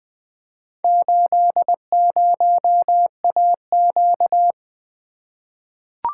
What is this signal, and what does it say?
Morse code, heard as a keyed tone, about 21 wpm, 700 Hz. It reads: 80AQ
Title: Four Letter-Number Combinations - 20wpm